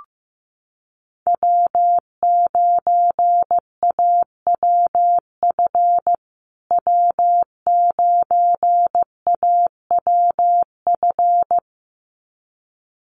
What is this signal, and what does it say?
Morse code, heard as a keyed tone, about 15 wpm, 700 Hz. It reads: W9AWF W9AWF
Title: USA Call Signs Repeated 2x 15wpm